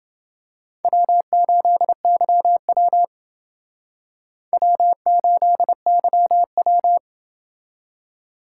Morse code, heard as a keyed tone, about 28 wpm, 700 Hz. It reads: W8YW W8YW